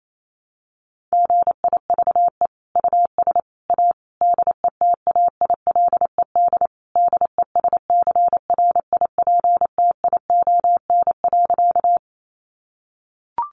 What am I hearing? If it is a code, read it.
Morse code, heard as a keyed tone, about 28 wpm, 700 Hz. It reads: GIVE US A DETAILED DESCRIPTION.
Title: Sentences from Top 1500 Words 28wpm